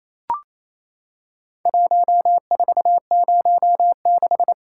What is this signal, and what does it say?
Morse code, heard as a keyed tone, about 27 wpm, 700 Hz. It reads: 1406
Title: Worked All Europe DX Contest - QTC Exchange - 28wpm